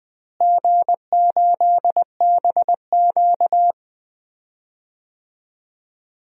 Morse code, heard as a keyed tone, about 20 wpm, 700 Hz. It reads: G8BQ
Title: DXCC-100 Call Signs - 4 Characters - Repeated 3x - 20wpm